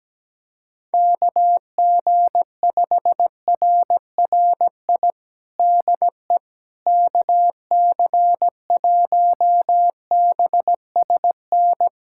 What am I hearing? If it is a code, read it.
Morse code, heard as a keyed tone, about 17 wpm, 700 Hz. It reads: KG5RRI DE KC1BSN
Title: USA Callsign DE Callsign 17wpm